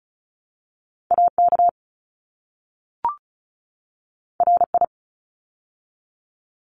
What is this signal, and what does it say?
Morse code, heard as a keyed tone, about 34 wpm, 700 Hz. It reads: AK RI
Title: ARRL Sections 35wpm